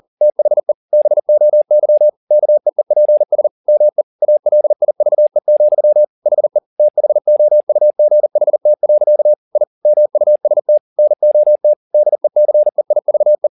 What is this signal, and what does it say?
Morse code, heard as a keyed tone, about 40 wpm, 600 Hz. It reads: THE BOY KEEPS ME ALIVE, HE THOUGHT. I MUST NOT DECEIVE
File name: the old man and the sea 2 40wpm